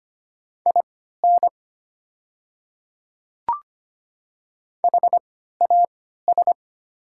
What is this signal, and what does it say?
Morse code, heard as a keyed tone, about 25 wpm, spaced as for 14 wpm, 700 Hz. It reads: IN HAS
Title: 3x Character Spacing - Top 100 Words - 25wpm